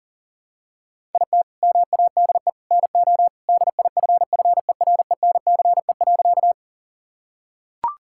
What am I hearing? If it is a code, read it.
Morse code, heard as a keyed tone, about 40 wpm, 700 Hz. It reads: IT MADE NO DIFFERENCE.
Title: Sentences from Top 1000 Words - Review Part 5 - 40wpm